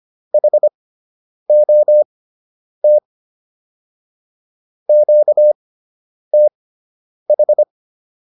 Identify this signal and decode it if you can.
Morse code, heard as a keyed tone, about 25 wpm, spaced as for 9 wpm, 600 Hz. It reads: HOT QTH